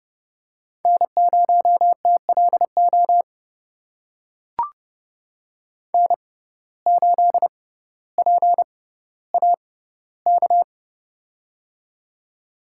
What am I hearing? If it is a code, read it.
Morse code, heard as a keyed tone, about 30 wpm, spaced as for 15 wpm, 700 Hz. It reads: N0TLO N8PAK
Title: USA Call Signs - 5 Characters - 6x Character Spacing - 30wpm